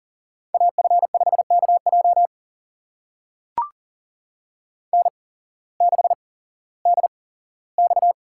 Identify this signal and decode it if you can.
Morse code, heard as a keyed tone, about 40 wpm, spaced as for 19 wpm, 700 Hz. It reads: AF5KJ N6DX
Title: USA Call Signs - 5 Characters - 8x Character Spacing - 40wpm